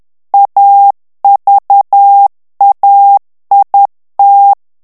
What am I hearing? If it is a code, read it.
Morse code, heard as a keyed tone, about 10 wpm, 800 Hz. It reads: AVAIT
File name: morse mp3 / FRANCAIS TEXTE CLAIR 2 VITESSE 10 MOTS MINUTES